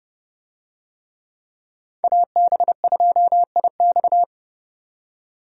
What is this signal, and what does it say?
Morse code, heard as a keyed tone, about 31 wpm, 700 Hz. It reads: AB2IX